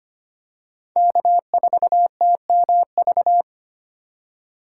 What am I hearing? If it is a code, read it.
Morse code, heard as a keyed tone, about 25 wpm, 700 Hz. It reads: K4TMV
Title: USA Call Signs - 5 Characters - 3x Character Spacing - 25wpm